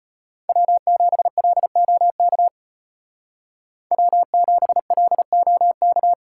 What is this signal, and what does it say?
Morse code, heard as a keyed tone, about 35 wpm, 700 Hz. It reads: W7LOK W7LOK